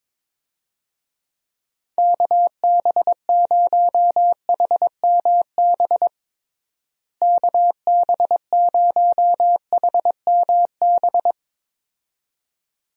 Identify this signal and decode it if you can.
Morse code, heard as a keyed tone, about 22 wpm, 700 Hz. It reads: KB0HMB KB0HMB
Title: USA Call Signs - 6 Characters - Repeated 2x - 22wpm